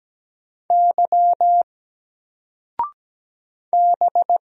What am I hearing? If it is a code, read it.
Morse code, heard as a keyed tone, about 17 wpm, 700 Hz. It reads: Y B
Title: Single Letter-Number 17wpm